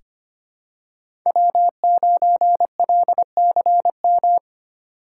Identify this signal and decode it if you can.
Morse code, heard as a keyed tone, about 25 wpm, 700 Hz. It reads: W9LCM
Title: USA Call Signs - 5 Characters - 3x Character Spacing - 25wpm